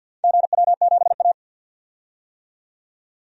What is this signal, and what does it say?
Morse code, heard as a keyed tone, about 48 wpm, 700 Hz. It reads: GW7A
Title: DXCC-100 Call Signs - 4 Characters - 50wpm